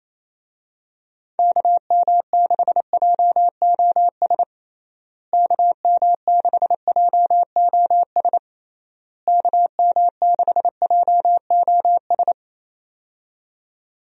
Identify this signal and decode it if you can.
Morse code, heard as a keyed tone, about 28 wpm, 700 Hz. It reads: KM6JOS KM6JOS KM6JOS
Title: USA Call Signs - 6 Characters - Repeated 3x - 28wpm